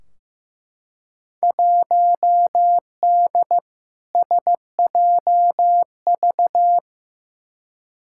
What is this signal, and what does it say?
Morse code, heard as a keyed tone, about 15 wpm, 700 Hz. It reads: 1D SJV
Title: Field Day Exchanges Encore - 15wpm